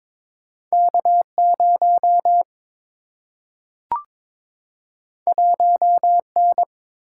Morse code, heard as a keyed tone, about 22 wpm, 700 Hz. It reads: K 0 1 N
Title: Single-Letter Single-Number 22wpm